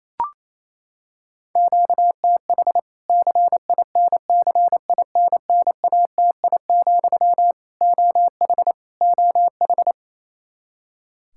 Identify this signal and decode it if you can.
Morse code, heard as a keyed tone, about 28 wpm, 700 Hz. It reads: QTH CINCINNATI, OH OH
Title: K1USN SST QSO Practice 28wpm